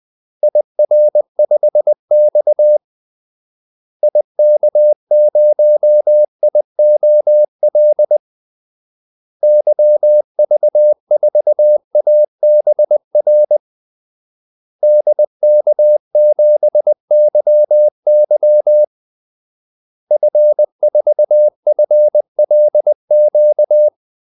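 Morse code, heard as a keyed tone, about 20 wpm, 600 Hz. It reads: IR5X IK0IOL YV4ABR DK7YY F4FLQ
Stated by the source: Morse call-sign practice at 20 wpm.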